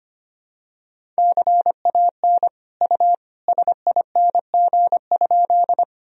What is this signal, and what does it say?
Morse code, heard as a keyed tone, about 25 wpm, 700 Hz. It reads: CAN U SING?
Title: Sentences from Top 400 Words and Abbreviations 25wpm